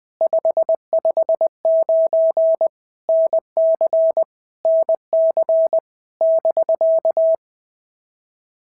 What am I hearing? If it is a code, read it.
Morse code, heard as a keyed tone, about 19 wpm, 650 Hz. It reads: 559 NC NC <BK>
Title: POTA QSO Practice - 20wpm